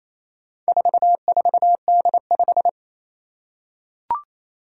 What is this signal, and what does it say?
Morse code, heard as a keyed tone, about 27 wpm, 700 Hz. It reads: 44D5
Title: Four Challenging Letter-Number Combinations 28wpm